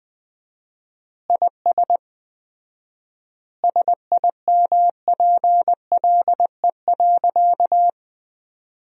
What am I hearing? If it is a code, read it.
Morse code, heard as a keyed tone, about 21 wpm, 700 Hz. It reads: IS SIMPLE.
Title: 4x Word-Spacing - Sentences from Top 400 Words - 20wpm